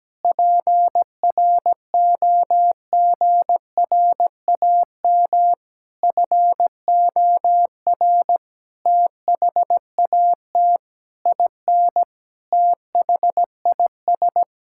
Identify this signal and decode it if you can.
Morse code, heard as a keyed tone, about 17 wpm, 700 Hz. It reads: PROGRAM FOR THAT IN THIS
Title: Sentences from Top 1000 Words - Review Part 5 - 17wpm